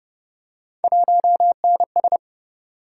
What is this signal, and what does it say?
Morse code, heard as a keyed tone, about 32 wpm, 700 Hz. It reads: 1NS